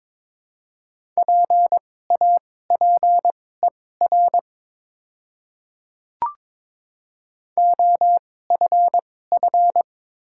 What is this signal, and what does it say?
Morse code, heard as a keyed tone, about 22 wpm, spaced as for 16 wpm, 700 Hz. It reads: PAPER OFF
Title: 2x Character Spacing - Top 500 Words - 22wpm